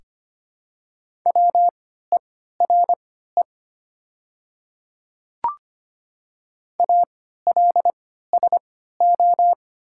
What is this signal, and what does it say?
Morse code, heard as a keyed tone, about 25 wpm, spaced as for 14 wpm, 690 Hz. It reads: WERE ALSO
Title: 3x Character Spacing - Top 200 Words - 25wpm